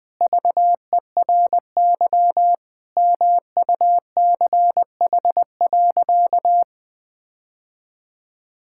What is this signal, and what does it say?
Morse code, heard as a keyed tone, about 21 wpm, 700 Hz. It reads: VERY MUCH.
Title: Sentences with 3-Word Phrases - 20wpm